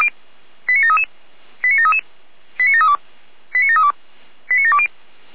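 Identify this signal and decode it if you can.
Five-tone selective call (ZVEI), heard as quick runs of five tones, tones of about 70 ms. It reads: 89733 89733 89732 89732 89720
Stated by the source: ZVEI five-tone call received by SDR.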